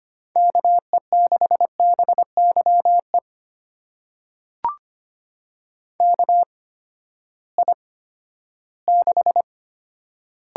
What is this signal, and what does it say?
Morse code, heard as a keyed tone, about 25 wpm, spaced as for 14 wpm, 700 Hz. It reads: KE6BYE KI6
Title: USA Call Signs - 6 Characters - 8x Character Spacing - 25wpm